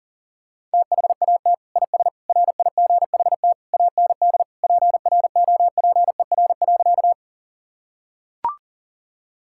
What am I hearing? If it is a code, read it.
Morse code, heard as a keyed tone, about 40 wpm, 700 Hz. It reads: THAT IS RIGHT AND PROPER.
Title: Sentences from Top 1000 Words - Review Part 5 - 40wpm